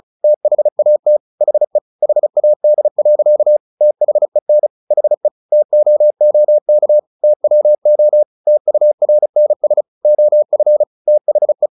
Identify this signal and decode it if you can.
Morse code, heard as a keyed tone, about 34 wpm, 600 Hz. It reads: THAT HE HAD. THEN HE TOOK TWO TURNS OF THE
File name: the old man and the sea 2 35wpm